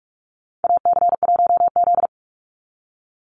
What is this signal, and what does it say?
Morse code, heard as a keyed tone, about 44 wpm, 700 Hz. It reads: AC1Z
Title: NA SOTA Activator Callsigns 45wpm